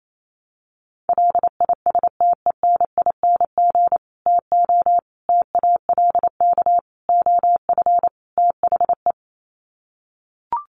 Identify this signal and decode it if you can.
Morse code, heard as a keyed tone, about 28 wpm, 700 Hz. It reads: LISTENING TO TALK OF THE
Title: Sets of 5 Words 28wpm